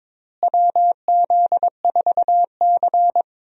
Morse code, heard as a keyed tone, about 22 wpm, 700 Hz. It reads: WZ4C
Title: USA Call Signs - 4 Characters - Repeated 2x - 22wpm